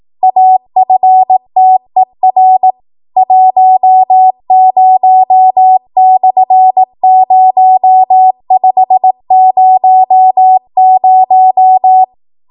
Morse code, heard as a keyed tone, about 18 wpm, 750 Hz. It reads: AFTER 10/0500
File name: Morse / 18WPM